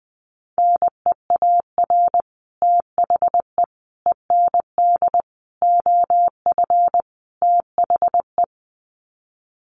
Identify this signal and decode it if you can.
Morse code, heard as a keyed tone, about 20 wpm, 700 Hz. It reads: NEAR THE END OF THE
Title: Sets of 5 Words Encore 20wpm